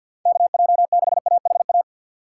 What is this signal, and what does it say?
Morse code, heard as a keyed tone, about 50 wpm, 700 Hz. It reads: KJ6AHA
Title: USA Call Signs - 2x Character Spacing - 50wpm